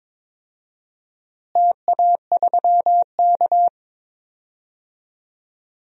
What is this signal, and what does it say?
Morse code, heard as a keyed tone, about 22 wpm, 700 Hz. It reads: TA3K
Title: DXCC-100 Call Signs - 4 Characters - Repeated 3x - 22wpm